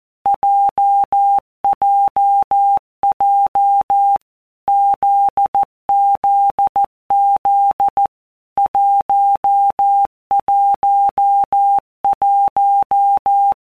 Morse code, heard as a keyed tone, about 14 wpm, 800 Hz. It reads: JJJ ZZZ 111
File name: Basic Letter Groups By WPM's / Groups@13 / JZ17@13